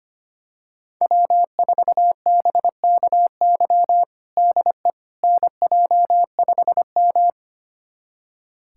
Morse code, heard as a keyed tone, about 25 wpm, 700 Hz. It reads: W4BKY DE NJ5M
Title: USA Callsign DE Callsign 25wpm